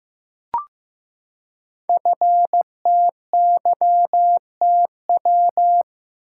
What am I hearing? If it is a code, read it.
Morse code, heard as a keyed tone, about 15 wpm, 700 Hz. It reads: FTYTW
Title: Code Groups - 5 Letters - 15wpm